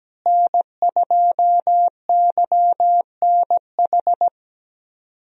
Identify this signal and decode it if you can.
Morse code, heard as a keyed tone, about 17 wpm, 700 Hz. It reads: N2YNH
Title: USA Call Signs - 5 Characters - Repeated 3x - 17wpm